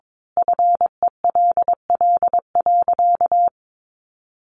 Morse code, heard as a keyed tone, about 22 wpm, 700 Hz. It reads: FELL.
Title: A Sound Of Thunder 22wpm